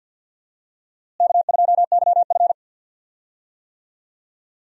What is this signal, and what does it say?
Morse code, heard as a keyed tone, about 52 wpm, 700 Hz. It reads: K2YF
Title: USA Call Signs - 4 Characters - Repeated 2x - 50wpm